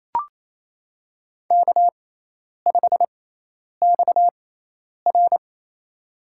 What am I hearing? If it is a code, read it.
Morse code, heard as a keyed tone, about 28 wpm, spaced as for 10 wpm, 700 Hz. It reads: K5XR
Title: USA Call Signs - 4 Characters - 6x Character Spacing - 28wpm